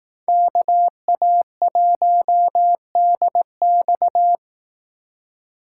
Morse code, heard as a keyed tone, about 18 wpm, 700 Hz. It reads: KA1DX
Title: Speed Racing - USA 2x2 Call Signs 15wpm